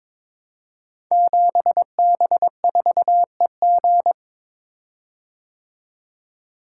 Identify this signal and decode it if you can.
Morse code, heard as a keyed tone, about 22 wpm, 700 Hz. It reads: 7B4EG
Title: Code Groups - 5 Letter:Numbers - 22wpm